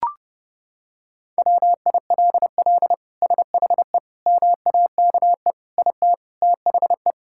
Morse code, heard as a keyed tone, about 30 wpm, 700 Hz. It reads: WILL SHE MAKE IT THE
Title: Sentences from Top 1000 Words - Review Part 5 - 30wpm